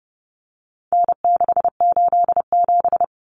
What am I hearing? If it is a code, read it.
Morse code, heard as a keyed tone, about 30 wpm, 700 Hz. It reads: N687
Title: Common Cut Numbers - 4-Digit 30wpm